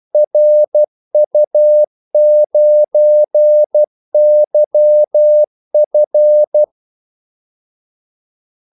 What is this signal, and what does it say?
Morse code, heard as a keyed tone, about 12 wpm, 600 Hz. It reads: RU9YF
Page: 2 ekstra ordmellemrum mellem kaldesignalerne